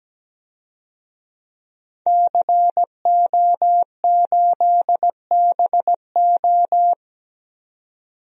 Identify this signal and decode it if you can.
Morse code, heard as a keyed tone, about 17 wpm, 700 Hz. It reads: CO8BO
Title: DXCC-100 Call Signs - 5 Characters - 17wpm